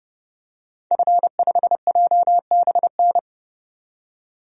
Morse code, heard as a keyed tone, about 30 wpm, 700 Hz. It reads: F5JBN
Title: DXCC-100 Call Signs - 5 Characters - 30wpm